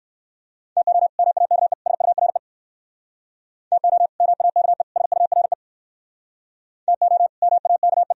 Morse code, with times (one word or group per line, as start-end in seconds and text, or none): 0.8-1.1 TO
1.2-1.7 MAKE
1.9-2.4 SURE
3.7-4.1 TO
4.2-4.8 MAKE
5.0-5.5 SURE
6.9-7.3 TO
7.4-8.1 MAKE